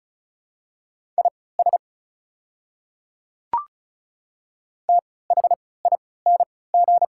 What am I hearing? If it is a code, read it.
Morse code, heard as a keyed tone, about 36 wpm, spaced as for 20 wpm, 700 Hz. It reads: IS THING
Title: 3x Character Spacing - Top 100 Words - 35wpm